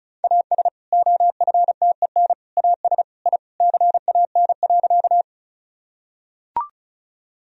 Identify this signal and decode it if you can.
Morse code, heard as a keyed tone, about 36 wpm, 700 Hz. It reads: AS OFTEN AS I CAN.
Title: Sentences from Top 1000 Words - Review Part 1 - 35wpm